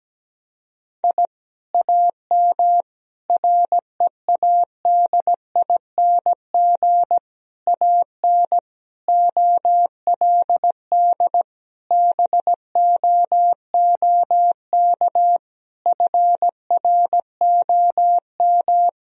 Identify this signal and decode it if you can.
Morse code, heard as a keyed tone, about 17 wpm, 700 Hz. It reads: I AM READING AN OLD BOOK FROM
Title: Sentences from Top 1000 Words - Review Part 4 - 17wpm